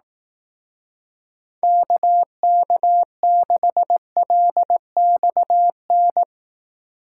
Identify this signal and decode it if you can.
Morse code, heard as a keyed tone, about 18 wpm, 700 Hz. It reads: KK6LXN
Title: Speed Racing - USA 2x3 Call Signs 15wpm